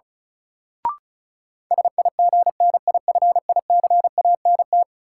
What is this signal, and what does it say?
Morse code, heard as a keyed tone, about 36 wpm, 700 Hz. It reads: SIGNIFICANT
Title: Sentences from Top 1600 Words - New Word Drill Alt - 35wpm